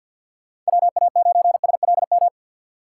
Morse code, heard as a keyed tone, about 48 wpm, 700 Hz. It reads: WA9SLM
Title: USA Call Signs - 2x Character Spacing - 50wpm